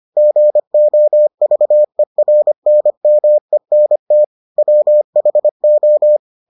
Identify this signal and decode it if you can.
Morse code, heard as a keyed tone, about 25 wpm, 600 Hz. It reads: GOVERNMENT WHO